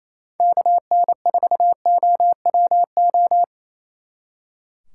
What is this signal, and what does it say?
Morse code, heard as a keyed tone, about 33 wpm, 700 Hz. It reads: KN4OWO
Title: Speed Racing - USA 2x3 Call Signs 28wpm